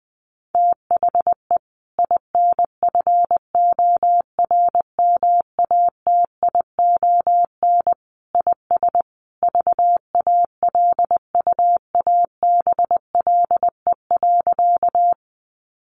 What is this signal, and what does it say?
Morse code, heard as a keyed tone, about 20 wpm, 700 Hz. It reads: THE INFORMATION IS VALUABLE.
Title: Sentences from Top 1100 Words 20wpm